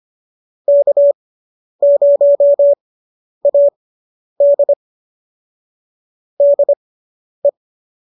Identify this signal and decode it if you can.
Morse code, heard as a keyed tone, about 25 wpm, spaced as for 10 wpm, 570 Hz. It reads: K0AD DE